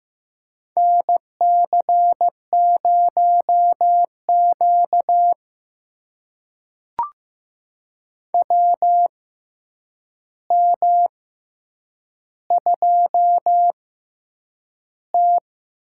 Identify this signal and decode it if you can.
Morse code, heard as a keyed tone, about 15 wpm, spaced as for 8 wpm, 700 Hz. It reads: NC0Q WM2T
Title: USA Call Signs - 4 Characters - 6x Character Spacing - 15wpm